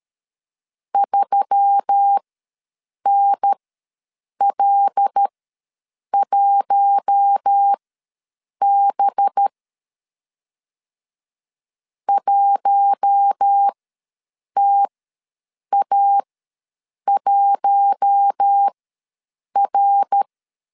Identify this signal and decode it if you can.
Morse code, heard as a keyed tone, about 13 wpm, spaced as for 7 wpm, 800 Hz. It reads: 3NL1B 1TA1R
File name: 5 LB ws 12wpm